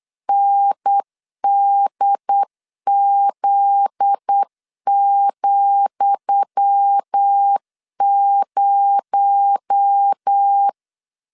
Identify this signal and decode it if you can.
Morse code, heard as a keyed tone, about 8 wpm, 800 Hz. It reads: NDZ,0
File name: LEZIONI / 10 SIGNS / 10D COMA 8wpm 3749